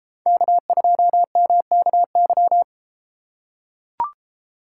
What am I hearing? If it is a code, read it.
Morse code, heard as a keyed tone, about 32 wpm, 700 Hz. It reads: K2MKY